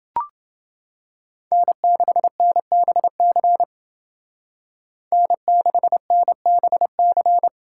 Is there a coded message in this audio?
Yes: Morse code from 1.5 to 7.5 s, heard as a keyed tone, about 28 wpm, 700 Hz. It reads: N6NBC N6NBC